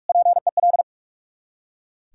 Morse code, heard as a keyed tone, about 44 wpm, 690 Hz. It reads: PEL
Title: Ontario QSO Party Exchange 45wpm